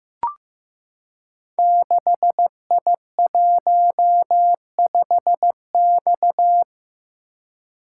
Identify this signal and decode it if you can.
Morse code, heard as a keyed tone, about 15 wpm, 700 Hz. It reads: 6I15X
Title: Code Groups - 5 Letter:Numbers - 15wpm